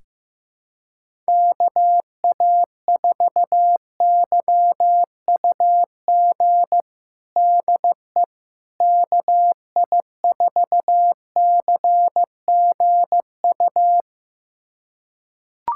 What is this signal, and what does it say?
Morse code, heard as a keyed tone, about 16 wpm, 700 Hz. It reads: KA4YUG DE KI4CGU
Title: USA Callsign DE Callsign - 6 Characters - 15wpm